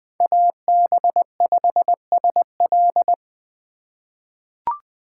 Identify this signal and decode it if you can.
Morse code, heard as a keyed tone, about 20 wpm, 700 Hz. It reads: AB5SL
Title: USA Call Signs - 5 Characters - 3x Character Spacing - 20wpm